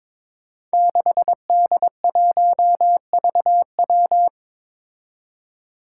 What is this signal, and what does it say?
Morse code, heard as a keyed tone, about 22 wpm, 700 Hz. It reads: 6D1VW
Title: Five Challenging Letter-Number Combinations 22wpm